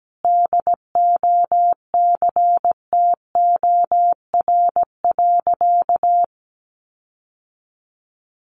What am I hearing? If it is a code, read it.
Morse code, heard as a keyed tone, about 17 wpm, 700 Hz. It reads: DOCTOR.
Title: Sentences from Top 1500 Words 17wpm